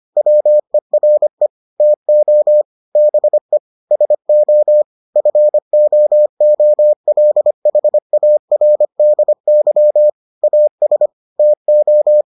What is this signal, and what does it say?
Morse code, heard as a keyed tone, about 25 wpm, 600 Hz. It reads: WERE TO BE SO FOOLHARDY AS TO